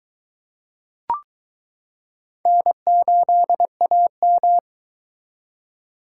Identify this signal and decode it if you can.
Morse code, heard as a keyed tone, about 22 wpm, 700 Hz. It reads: N8AM